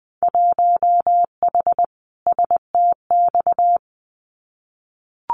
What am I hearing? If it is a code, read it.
Morse code, heard as a keyed tone, about 20 wpm, 700 Hz. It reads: 1H STX
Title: Winter Field Day Exchanges Encore 20wpm